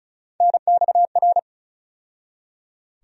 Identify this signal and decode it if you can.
Morse code, heard as a keyed tone, about 34 wpm, 700 Hz. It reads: NXR
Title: USA Call Sign Postfixes - 3 Letters 35wpm